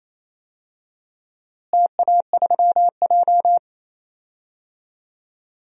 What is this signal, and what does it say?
Morse code, heard as a keyed tone, about 28 wpm, 700 Hz. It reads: TA3J
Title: DXCC-100 Call Signs - 4 Characters - Repeated 3x - 28wpm